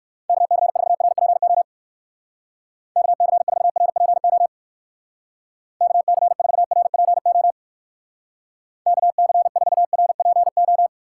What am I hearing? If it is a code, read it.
Morse code, heard as a keyed tone, about 58 wpm, 700 Hz. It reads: KC4RPO KC4RPO KC4RPO KC4RPO